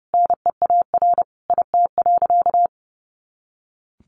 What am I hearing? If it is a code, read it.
Morse code, heard as a keyed tone, about 28 wpm, 700 Hz. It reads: NEAR IT.
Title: A Sound Of Thunder 30wpm